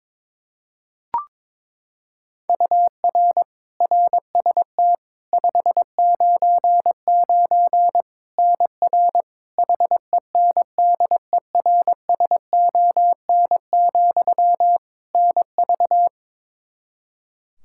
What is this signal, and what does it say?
Morse code, heard as a keyed tone, about 22 wpm, 700 Hz. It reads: UR RST 599 NR HENDERSON, NV
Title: Practice QSO - Exchange 1 - Beginning - 22wpm